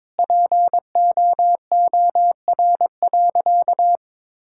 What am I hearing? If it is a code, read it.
Morse code, heard as a keyed tone, about 22 wpm, 700 Hz. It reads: POOR.
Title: Sentences from Top 1600 Words 22wpm